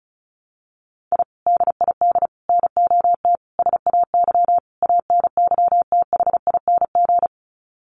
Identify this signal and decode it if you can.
Morse code, heard as a keyed tone, about 34 wpm, 700 Hz. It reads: I DID NOT SAY ANYTHING
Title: Sets of 5 Words Encore2 35wpm